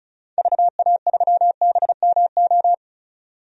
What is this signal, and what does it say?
Morse code, heard as a keyed tone, about 36 wpm, 700 Hz. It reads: VA3BMO
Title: DXCC-100 Call Signs - 6 Characters - 35wpm